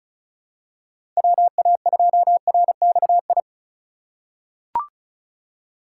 Morse code, heard as a keyed tone, about 36 wpm, 700 Hz. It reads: WA2RXI
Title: USA Call Signs - 6 Characters - 3x Character Spacing - 35wpm